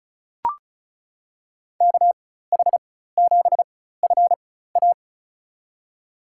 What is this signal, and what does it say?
Morse code, heard as a keyed tone, about 34 wpm, spaced as for 16 wpm, 700 Hz. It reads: KH7FA
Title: USA Call Signs - 5 Characters - 4x Character Spacing - 35wpm